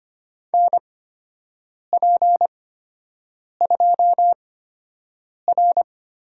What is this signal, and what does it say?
Morse code, heard as a keyed tone, about 25 wpm, spaced as for 7 wpm, 700 Hz. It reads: NP2R
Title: USA Call Signs - 8x Character Spacing - 25wpm